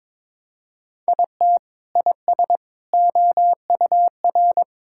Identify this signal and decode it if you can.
Morse code, heard as a keyed tone, about 22 wpm, 700 Hz. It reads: IT IS OUR
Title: Sentences from Top 1000 Words - Review Part 6 - 22wpm